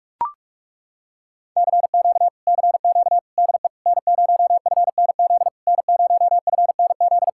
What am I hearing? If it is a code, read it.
Morse code, heard as a keyed tone, about 40 wpm, 700 Hz. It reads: CQ CQ DE N0FNZ N0FNZ